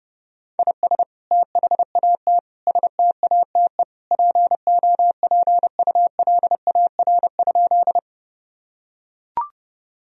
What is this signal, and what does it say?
Morse code, heard as a keyed tone, about 32 wpm, 700 Hz. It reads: IS THAT STATE POPULAR?
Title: Sentences from Top 1000 Words - Review Part 3 - 30wpm